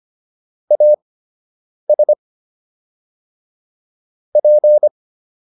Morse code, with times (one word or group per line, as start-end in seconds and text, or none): 0.7-2.1 AS
4.4-4.9 P